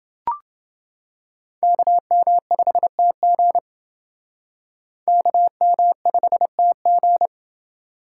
Morse code, heard as a keyed tone, about 28 wpm, 700 Hz. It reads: KM5TG KM5TG